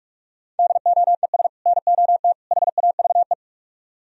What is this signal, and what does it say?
Morse code, heard as a keyed tone, about 46 wpm, 700 Hz. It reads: DOES NOT HAVE